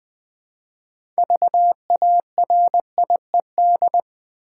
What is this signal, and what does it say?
Morse code, heard as a keyed tone, about 20 wpm, 700 Hz. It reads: VARIED
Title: 6 Letter Words - First 3 Letters Given - 20wpm